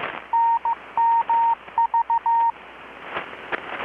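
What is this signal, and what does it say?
Morse code, heard as a keyed tone, about 15 wpm, 920 Hz. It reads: NMV
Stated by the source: Morse from KPH's Night of Nights XX maritime radiotelegraphy event, received in Chiapas, Mexico about 3,800 km away on a Kenwood R-600 with a grounded 10 m long-wire antenna.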